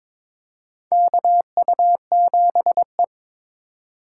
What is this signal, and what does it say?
Morse code, heard as a keyed tone, about 22 wpm, 700 Hz. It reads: KU7E
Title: USA Call Signs - 4 Characters - 2x Character Spacing - 22wpm